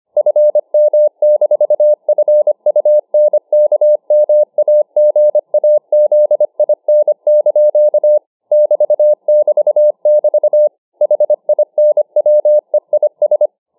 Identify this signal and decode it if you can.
Morse code, heard as a keyed tone, about 25 wpm, 600 Hz. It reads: FM-FUNKMAGAZIN) <BT><BT><BT> HINWEIS